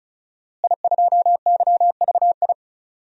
Morse code, heard as a keyed tone, about 34 wpm, 690 Hz. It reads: I2YVI
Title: DXCC-100 Call Signs - 5 Characters - Repeated 3x - 35wpm